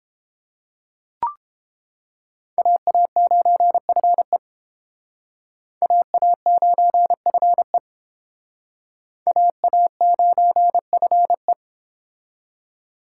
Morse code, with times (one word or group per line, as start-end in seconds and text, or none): 2.6-4.4 AA9FE
5.8-7.8 AA9FE
9.3-11.5 AA9FE